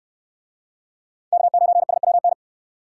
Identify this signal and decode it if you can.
Morse code, heard as a keyed tone, about 68 wpm, 700 Hz. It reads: K9SWN